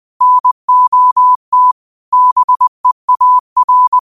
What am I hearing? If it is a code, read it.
Morse code, heard as a keyed tone, about 20 wpm, 1000 Hz. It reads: NOT BEAR
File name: qotdfiles / 20wpm